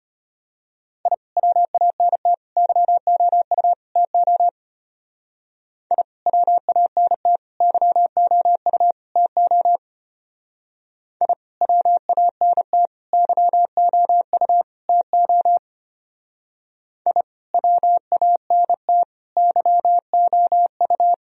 Morse code, heard as a keyed tone, about 31 wpm, 700 Hz. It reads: I WANT YOU TO I WANT YOU TO I WANT YOU TO I WANT YOU